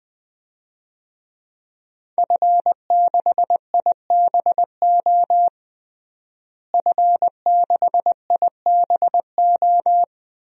Morse code, heard as a keyed tone, about 20 wpm, 700 Hz. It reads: F6IBO F6IBO
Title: DXCC-100 Call Signs - 5 Characters - Repeated 2x - 20wpm